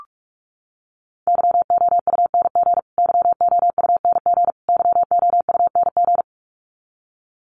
Morse code, heard as a keyed tone, about 44 wpm, 700 Hz. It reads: YOUNG YOUNG YOUNG
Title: Top 500 Words Repeated 45wpm